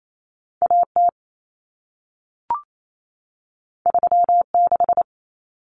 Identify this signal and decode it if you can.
Morse code, heard as a keyed tone, about 28 wpm, 700 Hz. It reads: AT 36
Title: Common Cut Numbers - 2-Digit 28wpm